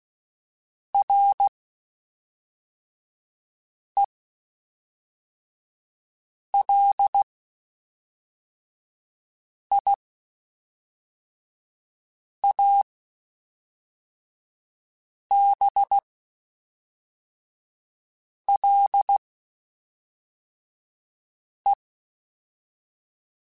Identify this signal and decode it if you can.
Morse code, heard as a keyed tone, about 16 wpm, spaced as for 3 wpm, 800 Hz. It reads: RELIABLE
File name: (spaced) / QSOs Spaced-mp3s / WD0FBT 1 @5 sp